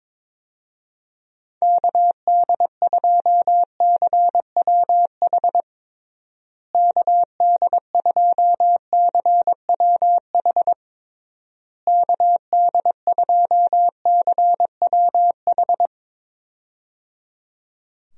Morse code, heard as a keyed tone, about 22 wpm, 700 Hz. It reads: KD2CWH KD2CWH KD2CWH
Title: USA Call Signs - 6 Characters - Repeated 3x - 22wpm